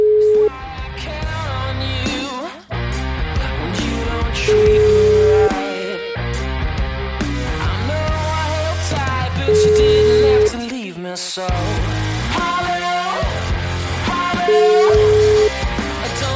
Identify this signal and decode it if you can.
Ringback tone (Europe, 425 Hz), with noise.